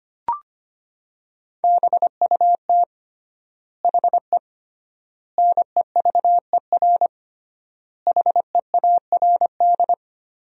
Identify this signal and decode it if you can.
Morse code, heard as a keyed tone, about 25 wpm, 700 Hz. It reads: BUT HE NEVER HEARD
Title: 3x Word-Spacing - Sentences from Top 400 Words - 25wpm